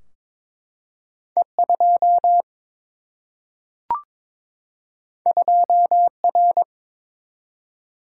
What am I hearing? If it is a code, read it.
Morse code, heard as a keyed tone, about 22 wpm, 700 Hz. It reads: E 2 2 R
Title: Single-Letter Single-Number 22wpm